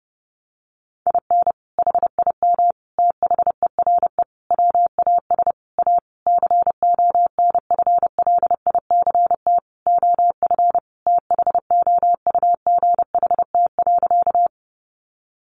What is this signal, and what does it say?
Morse code, heard as a keyed tone, about 30 wpm, 700 Hz. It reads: IN HIM THERE WAS A CONFLICT OF THOUGHT.
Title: Sentences from Top 1300 Words 30wpm